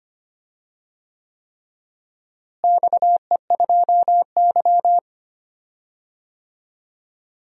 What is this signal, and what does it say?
Morse code, heard as a keyed tone, about 25 wpm, 700 Hz. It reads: XE2Y
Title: DXCC-100 Call Signs - 4 Characters - Repeated 3x - 25wpm